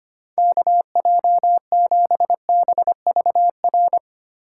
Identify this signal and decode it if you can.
Morse code, heard as a keyed tone, about 25 wpm, 700 Hz. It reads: KJ7BVR
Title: USA Call Signs - 6 Characters - 3x Character Spacing - 25wpm